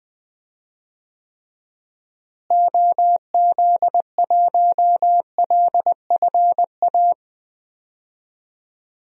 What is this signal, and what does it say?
Morse code, heard as a keyed tone, about 21 wpm, 700 Hz. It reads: OZ1LFA
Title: DXCC-100 Call Signs - 6 Characters - Repeated 3x - 20wpm